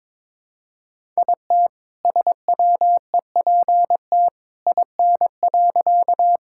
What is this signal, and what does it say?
Morse code, heard as a keyed tone, about 22 wpm, 700 Hz. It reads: IT SWEPT IN.
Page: morse-code-ninja-practice-22wpm